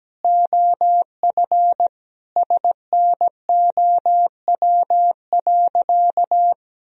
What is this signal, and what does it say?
Morse code, heard as a keyed tone, about 17 wpm, 700 Hz. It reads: OF SNOW.
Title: Sentences from Top 1000 Words - Review Part 3 - 17wpm